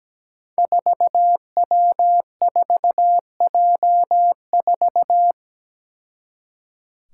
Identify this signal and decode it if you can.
Morse code, heard as a keyed tone, about 17 wpm, 700 Hz. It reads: 4W4J4
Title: Five Challenging Letter-Number Combinations 17wpm